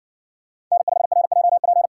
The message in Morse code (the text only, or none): N5RJP